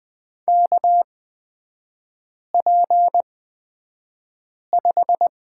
Morse code, heard as a keyed tone, about 20 wpm, spaced as for 5 wpm, 700 Hz. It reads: KP5
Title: 255 Lesson 41e - Period - Three Characters 20wpm